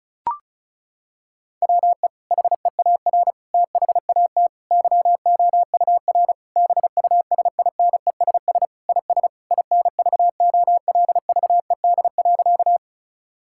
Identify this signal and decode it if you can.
Morse code, heard as a keyed tone, about 34 wpm, 700 Hz. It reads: WE HEAR THAT YOUR BUSINESS IS INVOLVED.
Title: Sentences from Top 1000 Words - Review Part 5 - 35wpm